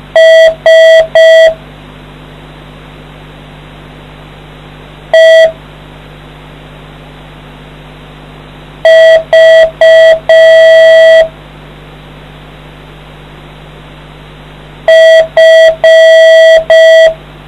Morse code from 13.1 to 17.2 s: F